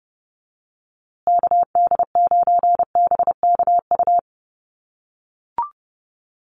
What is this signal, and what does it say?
Morse code, heard as a keyed tone, about 30 wpm, 700 Hz. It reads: KD9BKU
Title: USA Call Signs Repeated 2x 30wpm